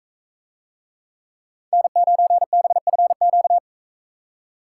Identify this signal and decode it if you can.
Morse code, heard as a keyed tone, about 44 wpm, 700 Hz. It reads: N9BFQ